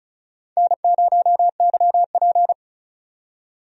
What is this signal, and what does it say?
Morse code, heard as a keyed tone, about 34 wpm, 700 Hz. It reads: N0YP
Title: USA Call Signs - 4 Characters - 35wpm